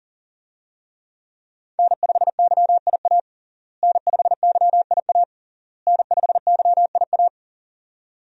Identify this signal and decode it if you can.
Morse code, heard as a keyed tone, about 40 wpm, 700 Hz. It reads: N5YIA N5YIA N5YIA
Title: USA Call Signs - 5 Characters - Repeated 3x - 40wpm